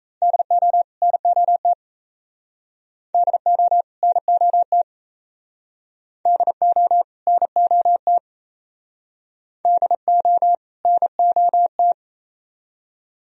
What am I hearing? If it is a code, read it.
Morse code, heard as a keyed tone, about 38 wpm, 690 Hz. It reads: DO NOT DO NOT DO NOT DO NOT